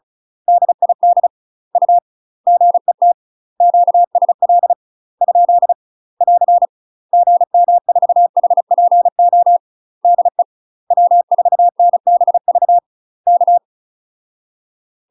Morse code, heard as a keyed tone, about 34 wpm, 700 Hz. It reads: DID U GET QSL ? <AR> GM4HPO DE W4NBV K
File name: UK MASTER QSOs TRAINING / 35WPM / UK MASTER QSOs 0030